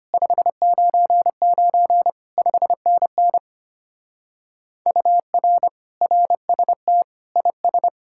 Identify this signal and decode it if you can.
Morse code, heard as a keyed tone, about 30 wpm, 700 Hz. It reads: 599 5NN UR RST IS
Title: Speed-Racing Common QSO Phrases - 25wpm